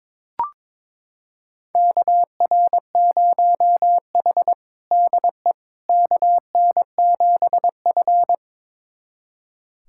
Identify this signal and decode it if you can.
Morse code, heard as a keyed tone, about 22 wpm, 700 Hz. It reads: KR0H DE KN7F
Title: USA Callsign DE Callsign - 4 Characters - 22wpm